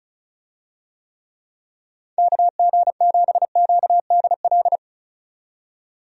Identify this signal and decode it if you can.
Morse code, heard as a keyed tone, about 34 wpm, 700 Hz. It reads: KG7QDL
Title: USA Call Signs - 6 Characters - 35wpm